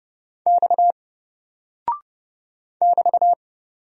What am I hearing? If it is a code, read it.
Morse code, heard as a keyed tone, about 31 wpm, 700 Hz. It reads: X <BT>
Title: Single Character (Letter:Number:Punctuation:Prosign) 30wpm